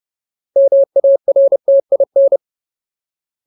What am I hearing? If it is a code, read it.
Morse code, heard as a keyed tone, about 30 wpm, 550 Hz. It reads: MARTIN
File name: Ultimate Word List / 30WPM Ultimate